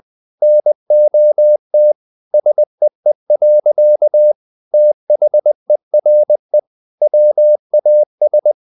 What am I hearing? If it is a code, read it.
Morse code, heard as a keyed tone, about 20 wpm, 600 Hz. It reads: NOT SEE. THERE WAS
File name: the old man and the sea part 2 20wpm